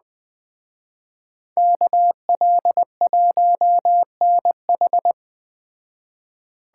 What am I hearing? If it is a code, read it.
Morse code, heard as a keyed tone, about 20 wpm, 700 Hz. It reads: KL1NH
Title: Speed Racing - USA 2x2 Call Signs 17wpm